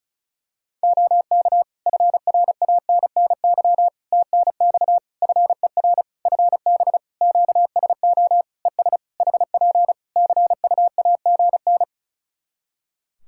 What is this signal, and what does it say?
Morse code, heard as a keyed tone, about 36 wpm, 700 Hz. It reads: OK FRANNY TNX FER FB QSO ES HP CUAGN
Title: Practice QSO - Exchange 4 - Ending - 35wpm